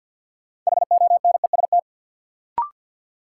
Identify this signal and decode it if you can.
Morse code, heard as a keyed tone, about 51 wpm, 700 Hz. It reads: HONEST
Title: Sets of 5 Words Encore - New Word Drill - 50wpm